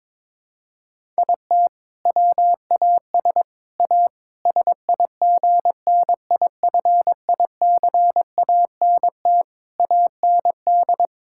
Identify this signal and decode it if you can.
Morse code, heard as a keyed tone, about 22 wpm, 700 Hz. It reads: IT WAS A SIGNIFICANT AND
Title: Sentences from Top 1000 Words - Review Part 6 - 22wpm